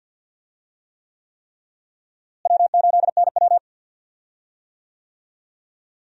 Morse code, heard as a keyed tone, about 50 wpm, 700 Hz. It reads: W8NW
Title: USA Call Signs - 4 Characters - Repeated 3x - 50wpm